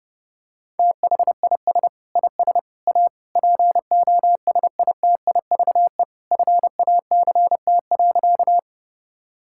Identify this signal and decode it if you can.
Morse code, heard as a keyed tone, about 30 wpm, 700 Hz. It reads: THIS IS A POSITIVE FACT.
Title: Sentences from Top 1000 Words - Review Part 5 - 30wpm